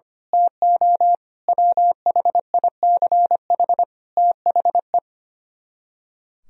Morse code, heard as a keyed tone, about 24 wpm, 700 Hz. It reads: TO WHICH THE
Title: Speed-Racing - Sets of 5 Words 25wpm